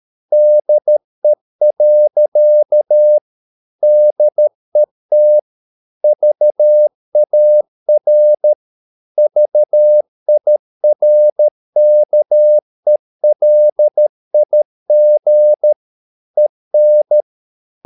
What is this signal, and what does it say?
Morse code, heard as a keyed tone, about 13 wpm, 600 Hz. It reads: DE. DET VAR VIRKELIG EN
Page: Ferie 13 wpm